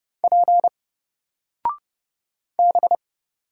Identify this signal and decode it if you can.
Morse code, heard as a keyed tone, about 31 wpm, 700 Hz. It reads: P B
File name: Single-Letter-Number-30wpm